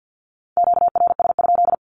Morse code, heard as a keyed tone, about 48 wpm, 700 Hz. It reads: QRS?
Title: Q-Codes 50wpm